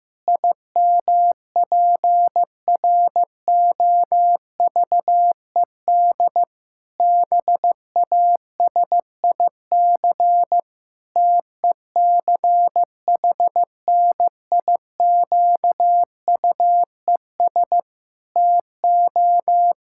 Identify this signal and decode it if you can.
Morse code, heard as a keyed tone, about 15 wpm, 700 Hz. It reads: IMPROVED BASIC TECHNIQUES TO
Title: Pangram Challenge v2 - 15wpm